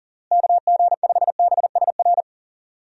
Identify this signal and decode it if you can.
Morse code, heard as a keyed tone, about 37 wpm, 700 Hz. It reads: KG5BSR
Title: USA Call Signs - 6 Characters - Repeated 3x - 40wpm